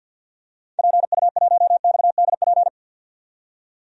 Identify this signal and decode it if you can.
Morse code, heard as a keyed tone, about 52 wpm, 700 Hz. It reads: PU1XDP
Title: DXCC-100 Call Signs - 6 Characters - 50wpm